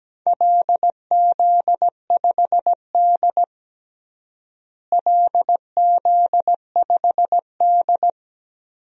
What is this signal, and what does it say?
Morse code, heard as a keyed tone, about 17 wpm, 700 Hz. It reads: LZ5D LZ5D
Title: DXCC-100 Call Signs - 4 Characters - Repeated 2x - 17wpm